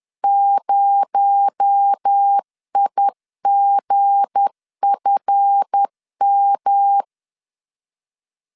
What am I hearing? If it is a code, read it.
Morse code, heard as a keyed tone, about 10 wpm, 800 Hz. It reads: 0IGFM
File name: LEZIONI / 7 GF9Z / 7 GF9 10wpm 2747